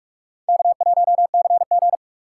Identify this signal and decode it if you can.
Morse code, heard as a keyed tone, about 41 wpm, 700 Hz. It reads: K1CG